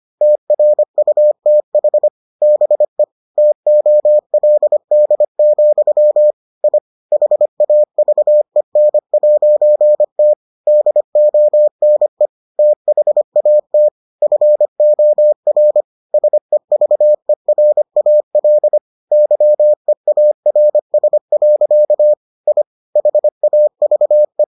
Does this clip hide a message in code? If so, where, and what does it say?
Yes: Morse code from 0.2 to 24.4 s, heard as a keyed tone, about 25 wpm, 600 Hz. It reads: TRUTH BE TOLD, I HAVEN'T DONE THAT FOR SEVERAL YEARS. I HAVE